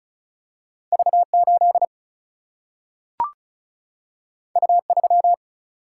Morse code, heard as a keyed tone, about 36 wpm, 700 Hz. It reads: V8 U3
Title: Single-Letter Single-Number 35wpm